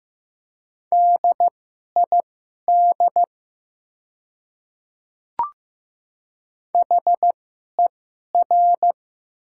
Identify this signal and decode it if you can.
Morse code, heard as a keyed tone, about 15 wpm, spaced as for 11 wpm, 700 Hz. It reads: DID HER
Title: 2x Character Spacing - Top 100 Words - 15wpm